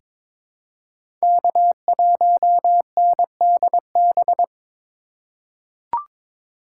Morse code, heard as a keyed tone, about 22 wpm, 700 Hz. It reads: K1NDB
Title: USA Call Signs - 5 Characters - 4x Character Spacing - 22wpm